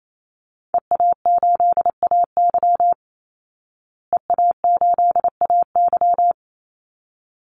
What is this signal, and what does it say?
Morse code, heard as a keyed tone, about 28 wpm, 700 Hz. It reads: EA8AY EA8AY
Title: Call Signs Repeated 3x - 28wpm